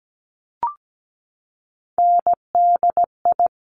Morse code, heard as a keyed tone, about 17 wpm, 700 Hz. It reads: NDI
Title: Top Three Letter Combinations in Words 17wpm